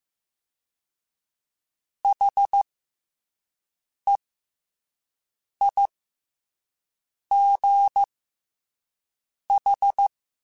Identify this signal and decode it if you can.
Morse code, heard as a keyed tone, about 15 wpm, spaced as for 5 wpm, 800 Hz. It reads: HEIGH
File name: Special INGs-mp3s / Special INGs-mp3s / SlowToFaster ing @10